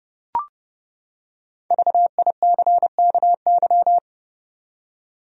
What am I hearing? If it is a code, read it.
Morse code, heard as a keyed tone, about 30 wpm, 700 Hz. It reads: VICKY
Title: USA Common Names - Max 5 Letters - 30wpm